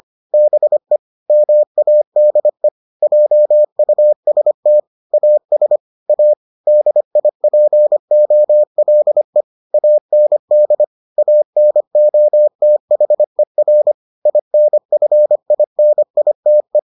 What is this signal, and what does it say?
Morse code, heard as a keyed tone, about 25 wpm, 600 Hz. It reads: BE MADE JUST AS A DIPOLE AND ANOTHER INFINITE